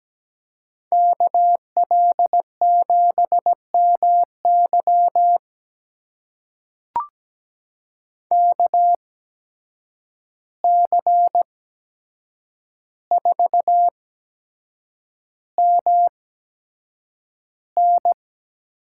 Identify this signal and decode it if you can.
Morse code, heard as a keyed tone, about 17 wpm, spaced as for 7 wpm, 700 Hz. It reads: KL7MY KC4MN
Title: USA Call Signs - 5 Characters - 8x Character Spacing - 17wpm